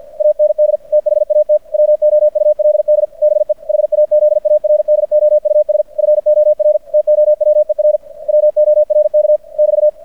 Morse code, heard as a keyed tone, about 50 wpm, 600 Hz. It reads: AND THAT WOULD BE HAZARDOUS FOR TOWER WORK <BT>
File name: archived / CRSnet